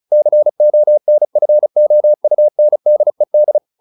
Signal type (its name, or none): Morse code